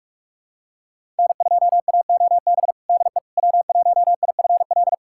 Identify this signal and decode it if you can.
Morse code, heard as a keyed tone, about 46 wpm, 700 Hz. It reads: N2AOB DE W1IFL
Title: USA Callsign DE Callsign - 5 Characters - 45wpm